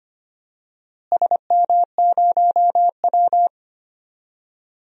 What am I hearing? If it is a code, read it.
Morse code, heard as a keyed tone, about 25 wpm, 700 Hz. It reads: SM0W
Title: DXCC-100 Call Signs - 4 Characters - Repeated 3x - 25wpm